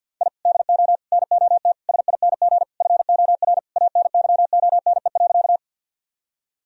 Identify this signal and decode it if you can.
Morse code, heard as a keyed tone, about 51 wpm, 700 Hz. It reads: I DO NOT SING FOR ANYONE.
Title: Sentences from Top 1000 Words - Review Part 2 - 50wpm